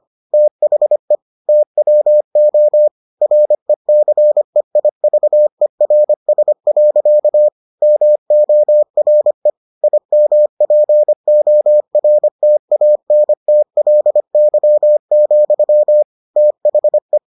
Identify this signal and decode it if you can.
Morse code, heard as a keyed tone, about 25 wpm, 600 Hz. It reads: THE TWO RECEIVERS. MORE IMPORTANTLY, THE